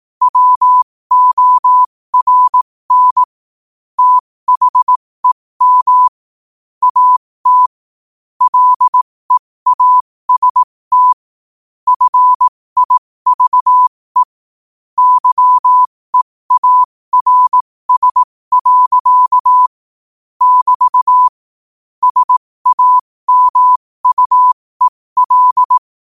Morse code, heard as a keyed tone, about 18 wpm, spaced as for 15 wpm, 1000 Hz. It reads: WORN THEM AT LEAST FIVE YEARS. <BT> SAMUEL
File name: qotdfiles / 15wpm